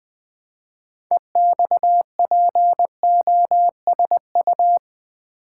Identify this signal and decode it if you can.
Morse code, heard as a keyed tone, about 20 wpm, 700 Hz. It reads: EXPOSU
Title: Build A Word - 2 Letters At A Time - 8-Letter Words - 20wpm